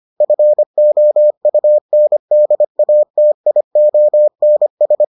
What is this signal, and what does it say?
Morse code, heard as a keyed tone, about 25 wpm, 600 Hz. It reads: FOUNDATIONS